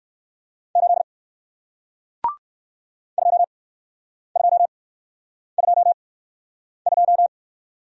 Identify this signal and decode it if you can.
Morse code, heard as a keyed tone, about 57 wpm, spaced as for 9 wpm, 700 Hz. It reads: 8 2222